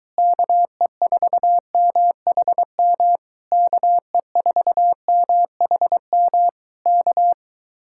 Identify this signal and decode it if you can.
Morse code, heard as a keyed tone, about 22 wpm, 700 Hz. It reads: KE4MHM KE4MHM K